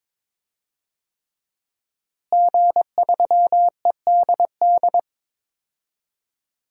Morse code, heard as a keyed tone, about 22 wpm, 700 Hz. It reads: G3EDD
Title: DXCC-100 Call Signs - 5 Characters - Repeated 3x - 22wpm